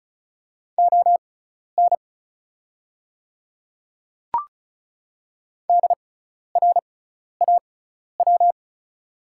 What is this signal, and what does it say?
Morse code, heard as a keyed tone, about 34 wpm, spaced as for 12 wpm, 690 Hz. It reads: ON DRAW
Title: 6x Character Spacing - Top 500 Words - 35wpm